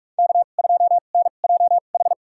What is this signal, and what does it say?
Morse code, heard as a keyed tone, about 46 wpm, spaced as for 33 wpm, 700 Hz. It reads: K2NJH
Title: USA Call Signs - 5 Characters - 2x Character Spacing - 45wpm